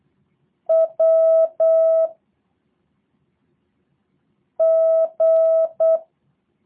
Morse code, heard as a keyed tone, about 8 wpm, 650 Hz. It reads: W G
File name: jh8cbh / 2023cw / review1-4